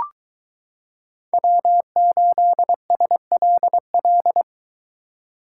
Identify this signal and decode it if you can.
Morse code, heard as a keyed tone, about 23 wpm, 700 Hz. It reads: W8SLL